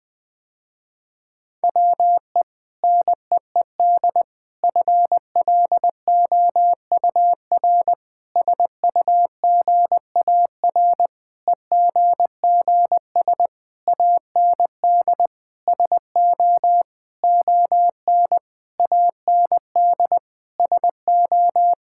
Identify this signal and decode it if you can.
Morse code, heard as a keyed tone, about 20 wpm, 700 Hz. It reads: WE NEED FLOUR SUGAR EGGS AND SO ON AND SO
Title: Sentences with 6-Word Phrases - Spoken Beginning - 20wpm